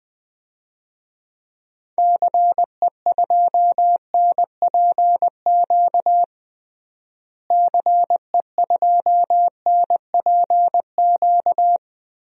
Morse code, heard as a keyed tone, about 20 wpm, 700 Hz. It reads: CE2NPQ CE2NPQ
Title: DXCC-100 Call Signs - 6 Characters - Repeated 3x - 20wpm